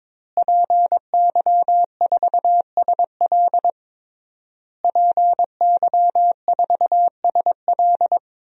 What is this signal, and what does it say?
Morse code, heard as a keyed tone, about 22 wpm, 700 Hz. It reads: PY4SL PY4SL
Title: DXCC-100 Call Signs - 5 Characters - Repeated 3x - 22wpm